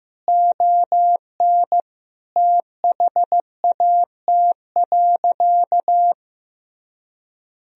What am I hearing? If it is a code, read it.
Morse code, heard as a keyed tone, about 15 wpm, 700 Hz. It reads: ON THAT.
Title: Sentences from Top 1000 Words - Review Part 1 - 15wpm